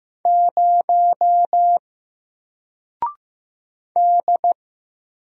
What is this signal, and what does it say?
Morse code, heard as a keyed tone, about 15 wpm, 700 Hz. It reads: 0 D
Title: Single Letter-Number Alt 15wpm